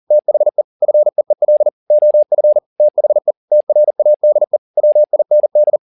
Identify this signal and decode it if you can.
Morse code, heard as a keyed tone, about 40 wpm, 600 Hz. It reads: THE FEEL OF THE TRADE WIND
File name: the old man and the sea 2 40wpm